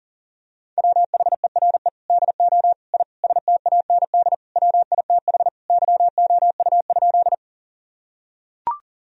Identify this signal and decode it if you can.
Morse code, heard as a keyed tone, about 37 wpm, 700 Hz. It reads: WHERE DO I STAND WITH YOU?
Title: Sentences from Top 1000 Words - Review Part 1 - 40wpm